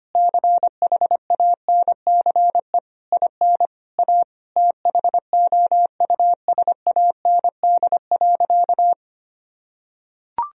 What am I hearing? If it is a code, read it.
Morse code, heard as a keyed tone, about 25 wpm, 700 Hz. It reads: CHANCE IN A THOUSAND.
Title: Sentences from Top 1000 Words - Review Part 3 - 25wpm